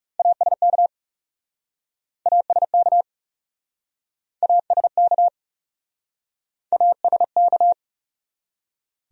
Morse code, heard as a keyed tone, about 40 wpm, 700 Hz. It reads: ASK ASK ASK ASK